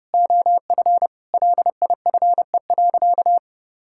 Morse code, heard as a keyed tone, about 32 wpm, 700 Hz. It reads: OF LIFE.
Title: Sentences from Top 1600 Words 30wpm